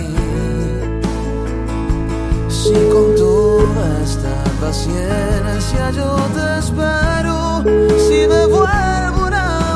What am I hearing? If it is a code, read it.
Ringback tone (Europe, 425 Hz), with noise.